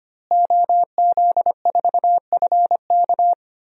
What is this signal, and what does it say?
Morse code, heard as a keyed tone, about 25 wpm, 700 Hz. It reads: OZ4FK
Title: DXCC-100 Call Signs - 5 Characters - Repeated 2x - 25wpm